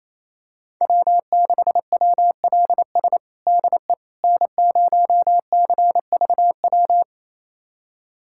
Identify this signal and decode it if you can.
Morse code, heard as a keyed tone, about 28 wpm, 700 Hz. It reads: W6WLS DE N0CVW
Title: USA Callsign DE Callsign - 5 Characters - 28wpm